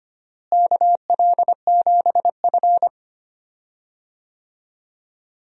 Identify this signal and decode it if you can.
Morse code, heard as a keyed tone, about 25 wpm, 700 Hz. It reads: KL7F
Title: DXCC-100 Call Signs - 4 Characters - Repeated 3x - 25wpm